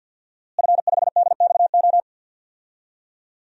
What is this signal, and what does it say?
Morse code, heard as a keyed tone, about 47 wpm, 700 Hz. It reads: F5DXO
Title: DXCC-100 Call Signs - 5 Characters - Repeated 3x - 50wpm